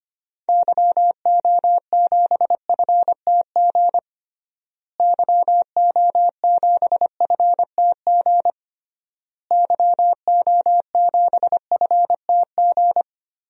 Morse code, heard as a keyed tone, about 25 wpm, 700 Hz. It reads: YO7FTG YO7FTG YO7FTG
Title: DXCC-100 Call Signs - 6 Characters - Repeated 3x - 25wpm